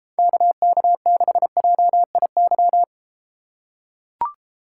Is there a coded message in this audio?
Yes: Morse code from 0.2 to 2.8 s, heard as a keyed tone, about 30 wpm, 700 Hz. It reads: KK6JIY